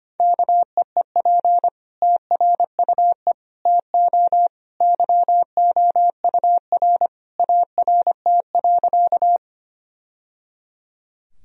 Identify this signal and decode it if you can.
Morse code, heard as a keyed tone, about 25 wpm, 700 Hz. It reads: KEEP TRUE TO YOUR ART.
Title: Sentences from Top 1000 Words - Review Part 3 - 25wpm